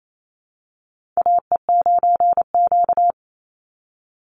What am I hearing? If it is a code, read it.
Morse code, heard as a keyed tone, about 28 wpm, 700 Hz. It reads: AE9Q
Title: NA SOTA Chaser Callsigns 28wpm